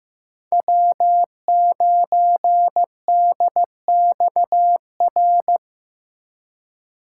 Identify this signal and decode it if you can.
Morse code, heard as a keyed tone, about 15 wpm, 700 Hz. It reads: W9DXR
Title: K1USN SST QSO Practice - 15wpm